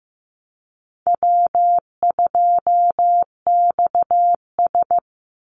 Morse code, heard as a keyed tone, about 15 wpm, 700 Hz. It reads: W2XS
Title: Easy Call Signs Repeated 3x - 15wpm